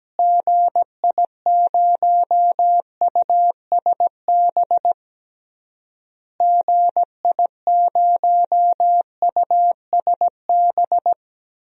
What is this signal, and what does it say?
Morse code, heard as a keyed tone, about 17 wpm, 700 Hz. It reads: GI0USB GI0USB
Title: DXCC-100 Call Signs - 6 Characters - Repeated 3x - 17wpm